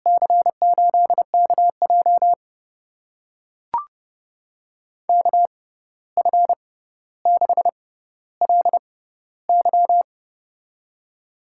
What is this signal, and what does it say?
Morse code, heard as a keyed tone, about 29 wpm, spaced as for 14 wpm, 700 Hz. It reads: C8KJ KF6LY
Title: USA Call Signs - 5 Characters - 6x Character Spacing - 30wpm